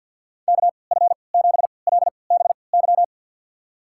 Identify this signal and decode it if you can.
Morse code, heard as a keyed tone, about 46 wpm, spaced as for 28 wpm, 700 Hz. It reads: KF7LBY
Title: USA Call Signs - 6 Characters - 3x Character Spacing - 50wpm